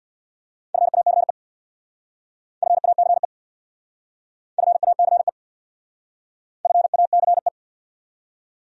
Morse code, heard as a keyed tone, about 64 wpm, 700 Hz. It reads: FACE FACE FACE FACE